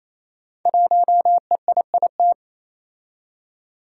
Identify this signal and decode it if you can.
Morse code, heard as a keyed tone, about 27 wpm, 700 Hz. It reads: 1EIIT
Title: Five Challenging Letter-Number Combinations 28wpm